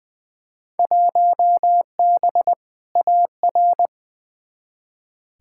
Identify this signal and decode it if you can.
Morse code, heard as a keyed tone, about 20 wpm, 700 Hz. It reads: 1B AR
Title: Field Day Exchanges Encore - 20wpm